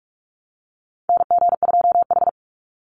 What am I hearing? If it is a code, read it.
Morse code, heard as a keyed tone, about 44 wpm, 700 Hz. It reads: NG2H
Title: Call Signs Repeated 3x - 45wpm